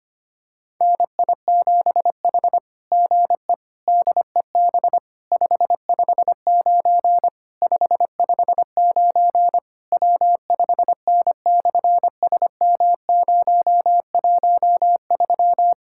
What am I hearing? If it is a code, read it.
Morse code, heard as a keyed tone, about 25 wpm, 700 Hz. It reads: NI7H GE DEB 559 559 W5N/SM013
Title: SOTA QSO Practice 25wpm